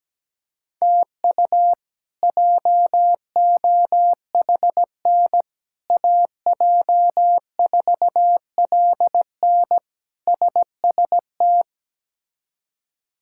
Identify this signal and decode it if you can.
Morse code, heard as a keyed tone, about 17 wpm, 700 Hz. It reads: TU JOHN AJ4LN SST
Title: K1USN SST QSO Practice - 17wpm